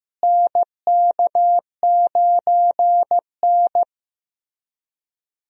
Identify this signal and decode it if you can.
Morse code, heard as a keyed tone, about 16 wpm, 700 Hz. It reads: NK9N
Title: USA Call Signs - 4 Characters - Repeated 3x - 15wpm